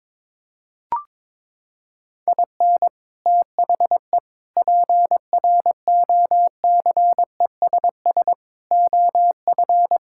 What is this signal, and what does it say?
Morse code, heard as a keyed tone, about 22 wpm, 700 Hz. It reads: IN THE PROCESS OF
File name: practice / 22wpm / Sentences with 4-Word Phrases - 22wpm